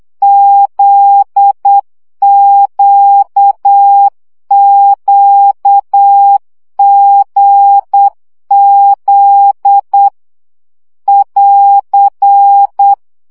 Morse code, heard as a keyed tone, about 8 wpm, 800 Hz. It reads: ZQQGZ <AR>
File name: CW gyakorlas g z q 7 baud